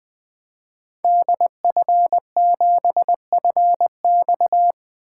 Morse code, heard as a keyed tone, about 20 wpm, 700 Hz. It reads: DF7FX
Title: 2023 SOTA Activator Call Signs - 20wpm